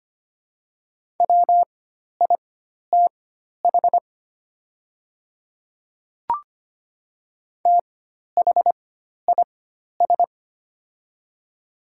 Morse code, heard as a keyed tone, about 25 wpm, spaced as for 12 wpm, 700 Hz. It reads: WITH THIS
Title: 4x Character Spacing - Top 100 Words - 25wpm